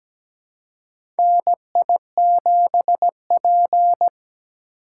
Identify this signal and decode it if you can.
Morse code, heard as a keyed tone, about 17 wpm, 700 Hz. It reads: NI7P
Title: USA Call Signs - 4 Characters - 3x Character Spacing - 17wpm